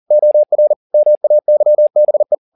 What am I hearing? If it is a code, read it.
Morse code, heard as a keyed tone, about 40 wpm, 600 Hz. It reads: OR MAYBE
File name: the old man and the sea 2 40wpm